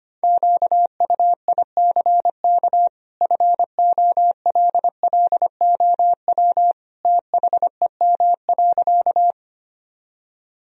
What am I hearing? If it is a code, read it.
Morse code, heard as a keyed tone, about 25 wpm, 700 Hz. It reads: QUICK FOLLOW THEM.
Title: Sentences from Top 1000 Words - Review Part 1 - 25wpm